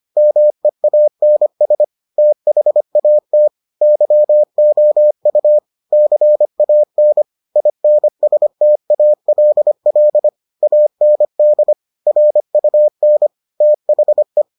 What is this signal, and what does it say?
Morse code, heard as a keyed tone, about 25 wpm, 600 Hz. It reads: MEANS THAT YOU CAN INSTALL AND RUN THE